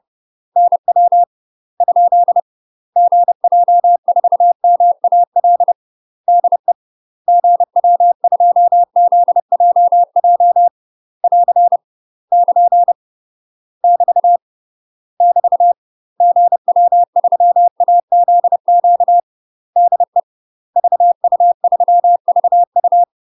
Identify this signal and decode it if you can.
Morse code, heard as a keyed tone, about 30 wpm, 700 Hz. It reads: NW ? GJ4MAL DE GW2ZJJ <AR> <KN> <BT> <BT> GW3AZQ DE VU3VU
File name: UK MASTER QSOs TRAINING / 30WPM / UK MASTER QSOs 0027